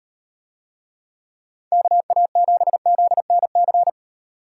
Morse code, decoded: KA7ZNC

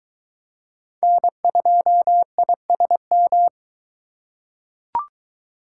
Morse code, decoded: N2ISM

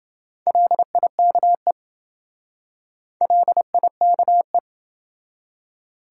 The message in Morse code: LIKE LIKE

